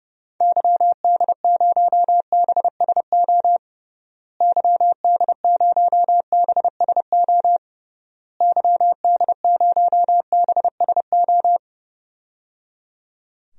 Morse code, heard as a keyed tone, about 30 wpm, 700 Hz. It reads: YD0BSO YD0BSO YD0BSO
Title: DXCC-100 Call Signs - 6 Characters - Repeated 3x - 30wpm